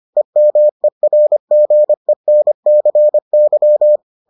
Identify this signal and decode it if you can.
Morse code, heard as a keyed tone, about 25 wpm, 600 Hz. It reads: EMERGENCY